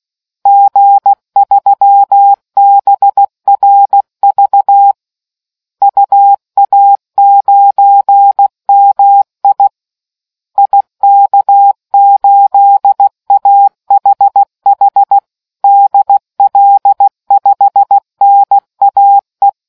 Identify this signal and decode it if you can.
Morse code, heard as a keyed tone, about 16 wpm, 800 Hz. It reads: G3BRV UA9MI IK8AHH DL5NAE